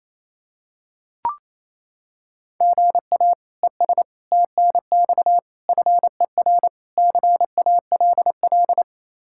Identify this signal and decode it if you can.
Morse code, heard as a keyed tone, about 28 wpm, 700 Hz. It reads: GA ES TNX FER CALL
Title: Practice QSO - Exchange 1 - Beginning - 28wpm